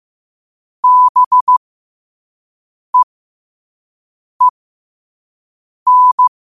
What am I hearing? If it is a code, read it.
Morse code, heard as a keyed tone, about 15 wpm, spaced as for 5 wpm, 1000 Hz. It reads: BEEN
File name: qotdfiles / 5wpm